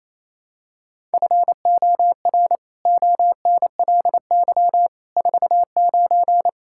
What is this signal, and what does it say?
Morse code, heard as a keyed tone, about 28 wpm, 700 Hz. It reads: FOR ONLY 49
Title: Pangram Challenge v2 - 28wpm